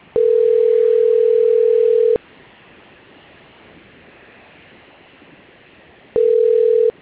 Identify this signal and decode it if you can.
Ringback tone (North America), 440 + 480 Hz.